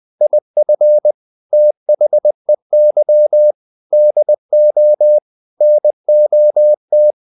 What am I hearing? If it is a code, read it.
Morse code, heard as a keyed tone, about 20 wpm, 600 Hz. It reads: IF THEY DO NOT
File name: the old man and the sea part 2 20wpm